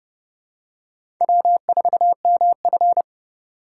Morse code, heard as a keyed tone, about 30 wpm, 700 Hz. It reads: W4MF
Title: USA Call Signs - 4 Characters - 4x Character Spacing - 30wpm